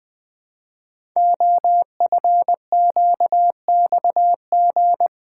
Morse code, heard as a keyed tone, about 20 wpm, 700 Hz. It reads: OFQXG
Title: Code Groups - 5 Letters - 20wpm